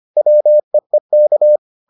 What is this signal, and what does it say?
Morse code, heard as a keyed tone, about 25 wpm, 600 Hz. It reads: WEEK